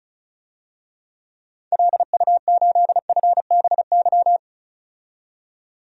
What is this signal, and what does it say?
Morse code, heard as a keyed tone, about 34 wpm, 700 Hz. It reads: LU8FBY
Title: DXCC-100 Call Signs - 6 Characters - 35wpm